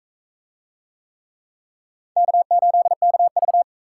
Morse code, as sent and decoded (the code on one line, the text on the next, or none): -.- ---.. -.- ...-
K8KV